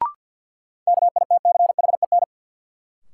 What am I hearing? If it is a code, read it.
Morse code, heard as a keyed tone, about 52 wpm, 700 Hz. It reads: KITCHEN
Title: Sentences from Top 1400 Words - New Word Drill - 50wpm